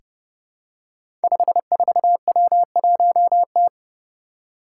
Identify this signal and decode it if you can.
Morse code, heard as a keyed tone, about 29 wpm, 700 Hz. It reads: 54W1T
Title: Five Challenging Letter-Number Combinations 30wpm